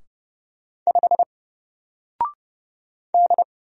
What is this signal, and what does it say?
Morse code, heard as a keyed tone, about 30 wpm, 700 Hz. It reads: 5 D
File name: Single-Letter-Number-30wpm